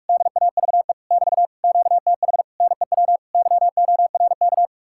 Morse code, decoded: DAVE <BT> QTH NEW YORK

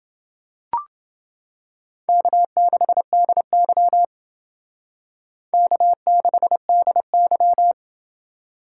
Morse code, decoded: K6DY K6DY

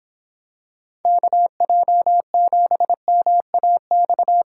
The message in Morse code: KJ7MAX

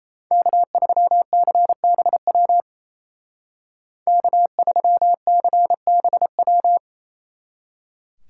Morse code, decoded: K3CBW K3CBW

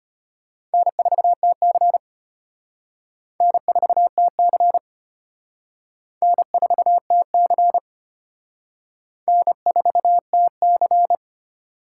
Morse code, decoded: N4TC N4TC N4TC N4TC

